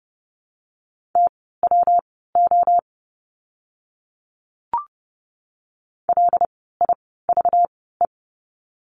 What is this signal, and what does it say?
Morse code, heard as a keyed tone, about 30 wpm, spaced as for 17 wpm, 700 Hz. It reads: TWO LIVE
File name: course / 30:10wpm / 237 Lesson 38f - Letter q - Words 30wpm